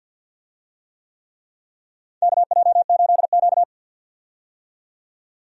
Morse code, heard as a keyed tone, about 46 wpm, 700 Hz. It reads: KJ8Q
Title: USA Call Signs - 4 Characters - 50wpm